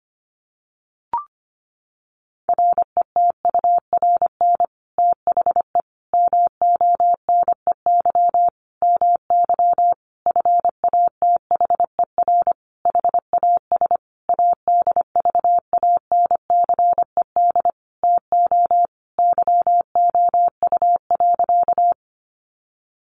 Morse code, heard as a keyed tone, about 25 wpm, 700 Hz. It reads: RETURN THE MONEY MY FATHER HAS ADVANCED TO YOU.
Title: Sentences from Top 1200 Words 25wpm